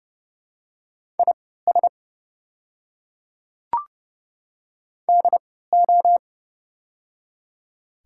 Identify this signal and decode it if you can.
Morse code, heard as a keyed tone, about 30 wpm, spaced as for 17 wpm, 700 Hz. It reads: IS DO
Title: 3x Character Spacing - Top 200 Words - 30wpm